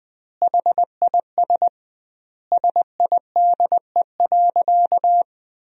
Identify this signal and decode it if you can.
Morse code, heard as a keyed tone, about 20 wpm, 700 Hz. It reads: HIS SIDE.
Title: 2x Word-Spacing - Sentences from Top 400 Words - 20wpm